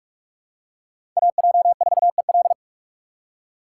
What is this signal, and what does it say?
Morse code, heard as a keyed tone, about 45 wpm, 700 Hz. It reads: AJ4EL
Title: USA Call Signs - 5 Characters - 4x Character Spacing - 45wpm